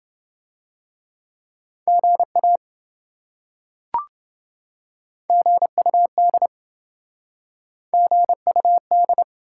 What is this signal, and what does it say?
Morse code, heard as a keyed tone, about 27 wpm, 700 Hz. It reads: GA GUD GUD